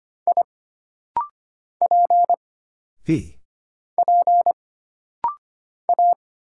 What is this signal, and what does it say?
Morse code, heard as a keyed tone, about 25 wpm, 700 Hz. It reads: I P P A
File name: Single Character (Letter:Number:Punctuation:Prosign) - Rapid-Fire - 25wpm